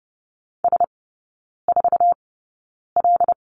Morse code, heard as a keyed tone, about 30 wpm, spaced as for 9 wpm, 700 Hz. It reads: S4L
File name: Lesson 16e - Letter c - Three Characters 30wpm